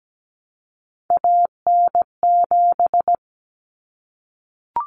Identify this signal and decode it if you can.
Morse code, heard as a keyed tone, about 17 wpm, 700 Hz. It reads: AN7
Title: Common Cut Numbers - 3-Digit 17wpm